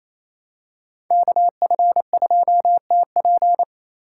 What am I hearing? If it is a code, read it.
Morse code, heard as a keyed tone, about 28 wpm, 700 Hz. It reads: KF2TP
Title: K1USN SST QSO Practice - 28wpm